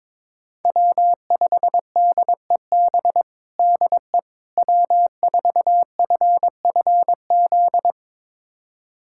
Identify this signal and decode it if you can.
Morse code, heard as a keyed tone, about 21 wpm, 700 Hz. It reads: W5DEB DE W4FFZ
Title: USA Callsign DE Callsign - 5 Characters - 22wpm